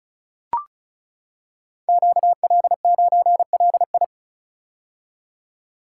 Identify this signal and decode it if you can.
Morse code, heard as a keyed tone, about 36 wpm, 700 Hz. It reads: QL9LI
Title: Code Groups - 5 Letter:Numbers - 35wpm